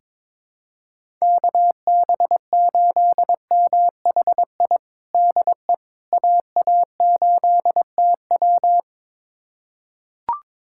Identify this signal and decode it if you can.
Morse code, heard as a keyed tone, about 22 wpm, 700 Hz. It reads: KB8MHI DE AA8TW
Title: USA Callsign DE Callsign 22wpm